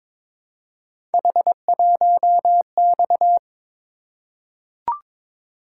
Morse code, heard as a keyed tone, about 22 wpm, 700 Hz. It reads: H1X
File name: practice / 22wpm / Three Letter-Number Combinations - 22wpm